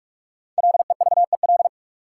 Morse code, heard as a keyed tone, about 46 wpm, 700 Hz. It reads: LEVEL
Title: Five Letter Words - 45wpm